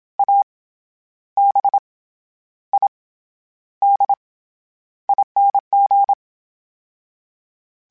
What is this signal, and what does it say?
Morse code, heard as a keyed tone, about 27 wpm, 800 Hz. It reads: A B I D ING
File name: Special INGs-mp3s / Special INGs-mp3s / SlowToFaster ing @25